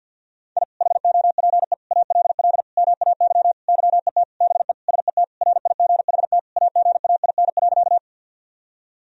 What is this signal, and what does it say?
Morse code, heard as a keyed tone, about 51 wpm, 700 Hz. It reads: I HOPE ALL MAY YET BE SET RIGHT AGAIN.
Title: Sentences from Top 1000 Words - Review Part 1 - 50wpm